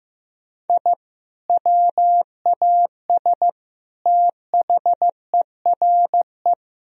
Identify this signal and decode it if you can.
Morse code, heard as a keyed tone, about 15 wpm, 700 Hz. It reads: I WAS THERE
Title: Sentences from Top 1000 Words - Review Part 2 - 15wpm